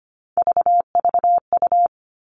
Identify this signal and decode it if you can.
Morse code, heard as a keyed tone, about 25 wpm, 700 Hz. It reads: VVU
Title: Three Challenging Letter-Number Combinations 25wpm